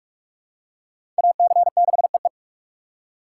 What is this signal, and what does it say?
Morse code, heard as a keyed tone, about 44 wpm, 700 Hz. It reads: AC6EE
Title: USA Call Signs - 5 Characters - 4x Character Spacing - 45wpm